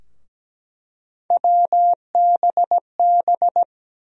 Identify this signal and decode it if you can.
Morse code, heard as a keyed tone, about 17 wpm, 700 Hz. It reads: WBB
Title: USA Call Sign Postfixes - 3 Letters 17wpm